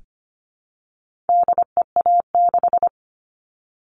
Morse code, heard as a keyed tone, about 25 wpm, 700 Hz. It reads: DEA6
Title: All Cut Numbers - 4-Digit 25wpm